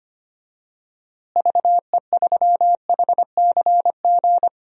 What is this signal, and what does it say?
Morse code, heard as a keyed tone, about 25 wpm, 700 Hz. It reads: VE3HCG
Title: DXCC-100 Call Signs - 6 Characters - Repeated 3x - 25wpm